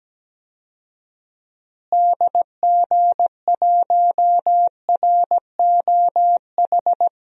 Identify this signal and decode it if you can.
Morse code, heard as a keyed tone, about 17 wpm, 700 Hz. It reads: DG1ROH
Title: DXCC-100 Call Signs - 6 Characters - Repeated 2x - 17wpm